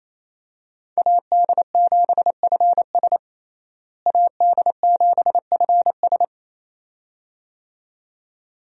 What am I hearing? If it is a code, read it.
Morse code, heard as a keyed tone, about 28 wpm, 700 Hz. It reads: AD7FS AD7FS
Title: USA Call Signs - 5 Characters - Repeated 2x - 28wpm